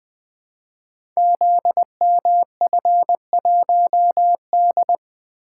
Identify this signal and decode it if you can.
Morse code, heard as a keyed tone, about 21 wpm, 700 Hz. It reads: ZMF1D
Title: Code Groups - 5 Letter:Numbers - 20wpm